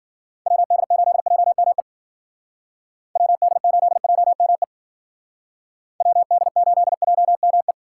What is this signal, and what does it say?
Morse code, heard as a keyed tone, about 53 wpm, 700 Hz. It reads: WD8JME WD8JME WD8JME